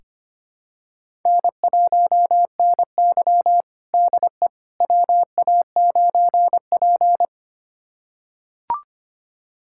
Morse code, heard as a keyed tone, about 25 wpm, 700 Hz. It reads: N1NY DE WA9P
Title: USA Callsign DE Callsign - 4 Characters - 25wpm